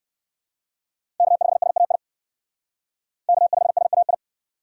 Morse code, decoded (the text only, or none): K5SAI K5SAI